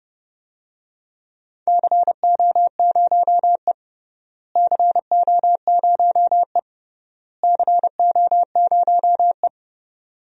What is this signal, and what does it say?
Morse code, heard as a keyed tone, about 29 wpm, 700 Hz. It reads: CO0E CO0E CO0E
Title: DXCC-100 Call Signs - 4 Characters - Repeated 3x - 30wpm